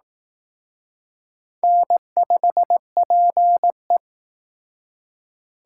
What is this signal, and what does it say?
Morse code, heard as a keyed tone, about 19 wpm, 700 Hz. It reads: N5PE